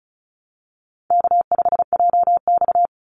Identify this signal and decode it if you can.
Morse code, heard as a keyed tone, about 34 wpm, 700 Hz. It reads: K5JX
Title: Easy Call Signs Repeated 2x - 35wpm